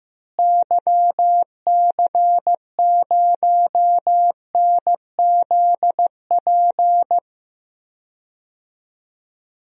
Morse code, heard as a keyed tone, about 16 wpm, 700 Hz. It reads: YC0NZP
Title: DXCC-100 Call Signs - 6 Characters - Repeated 2x - 15wpm